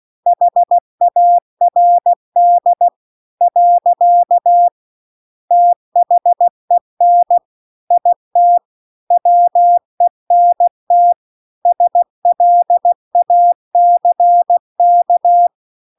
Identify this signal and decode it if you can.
Morse code, heard as a keyed tone, about 16 wpm, 700 Hz. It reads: HARD . THEN IT WENT SLACK
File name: CW-The-Big-Twoheaded-River-2-16WPM